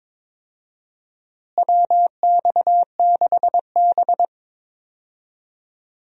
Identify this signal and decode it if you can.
Morse code, heard as a keyed tone, about 22 wpm, 700 Hz. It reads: WX6B
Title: USA Call Signs - 4 Characters - Repeated 2x - 22wpm